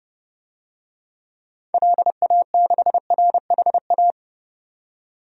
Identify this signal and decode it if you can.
Morse code, heard as a keyed tone, about 30 wpm, 700 Hz. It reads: LA6RHA
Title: DXCC-100 Call Signs - 6 Characters - 30wpm